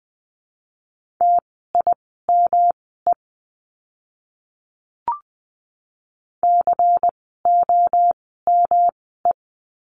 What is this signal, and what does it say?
Morse code, heard as a keyed tone, about 20 wpm, spaced as for 15 wpm, 700 Hz. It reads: TIME COME
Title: 237 Lesson 38f - Letter q - Words 20wpm